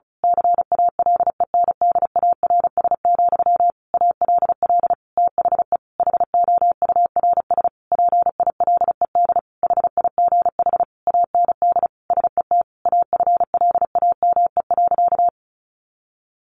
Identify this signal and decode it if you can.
Morse code, heard as a keyed tone, about 34 wpm, 700 Hz. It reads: CALENDARS, ALL THE HOURS PILED HIGH AND SET AFLAME.
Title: A Sound Of Thunder 35wpm